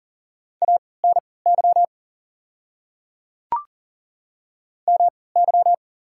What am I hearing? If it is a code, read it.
Morse code, heard as a keyed tone, about 40 wpm, spaced as for 23 wpm, 690 Hz. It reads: ANY MY
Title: 3x Character Spacing - Top 200 Words - 40wpm